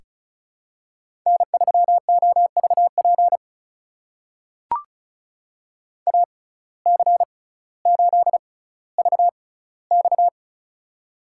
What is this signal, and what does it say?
Morse code, heard as a keyed tone, about 36 wpm, spaced as for 18 wpm, 700 Hz. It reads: N3OVP AC8VX
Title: USA Call Signs - 5 Characters - 6x Character Spacing - 35wpm